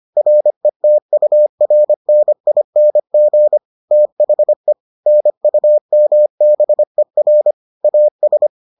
Morse code, heard as a keyed tone, about 25 wpm, 600 Hz. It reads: RETURNING THE NUMBER AS